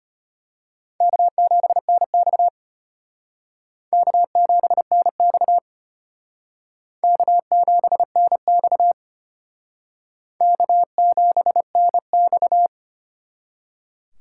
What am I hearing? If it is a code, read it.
Morse code, heard as a keyed tone, about 31 wpm, 700 Hz. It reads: K7NX K7NX K7NX K7NX
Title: Speed Racing - USA Call Signs 25wpm